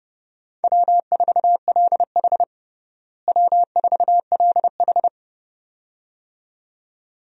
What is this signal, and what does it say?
Morse code, heard as a keyed tone, about 31 wpm, 700 Hz. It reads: W4LH W4LH
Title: USA Call Signs - 4 Characters - Repeated 2x - 30wpm